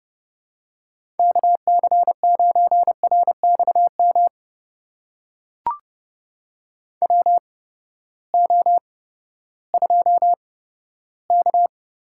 Morse code, heard as a keyed tone, about 30 wpm, spaced as for 15 wpm, 700 Hz. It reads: KC9RXM WO2K
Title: USA Call Signs - 8x Character Spacing - 30wpm